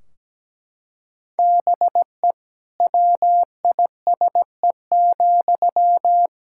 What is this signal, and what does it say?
Morse code, heard as a keyed tone, about 17 wpm, 700 Hz. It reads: BE WISE,
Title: Sentences from Top 1600 Words 17wpm